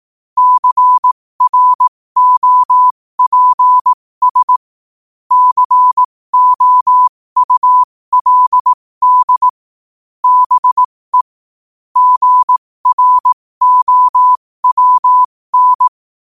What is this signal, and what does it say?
Morse code, heard as a keyed tone, about 18 wpm, spaced as for 15 wpm, 1000 Hz. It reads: CROPS COULD BE GROWN
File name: qotdfiles / 15wpm